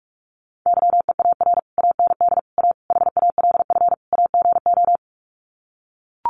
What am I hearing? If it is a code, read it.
Morse code, heard as a keyed tone, about 44 wpm, 700 Hz. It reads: YEAR AND A HALF AGO
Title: Sets of 5 Words Encore 45wpm